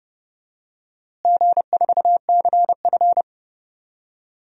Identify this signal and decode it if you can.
Morse code, heard as a keyed tone, about 31 wpm, 700 Hz. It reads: G4CF
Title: DXCC-100 Call Signs - 4 Characters - Repeated 3x - 30wpm